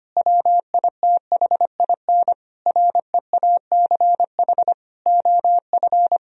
Morse code, heard as a keyed tone, about 25 wpm, 700 Hz. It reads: WITHIN REACH OF
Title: Sentences from Top 1000 Words - Review Part 4 - 25wpm